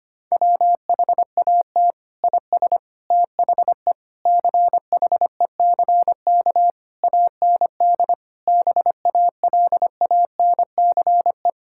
Morse code, heard as a keyed tone, about 25 wpm, 700 Hz. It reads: WHAT IS THE CHECK AND BALANCE
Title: Sentences from Top 1000 Words - Review Part 6 - 25wpm